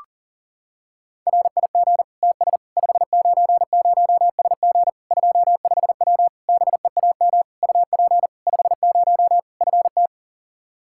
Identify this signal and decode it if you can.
Morse code, heard as a keyed tone, about 40 wpm, 700 Hz. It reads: RIG TS 590SG 25W BEAM UP 50 FT
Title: Practice QSO - Exchange 2 - Info - 40wpm